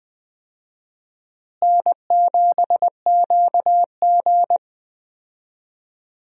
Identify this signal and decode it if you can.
Morse code, heard as a keyed tone, about 20 wpm, 700 Hz. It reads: N7QG
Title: USA Call Signs - 4 Characters - Repeated 2x - 20wpm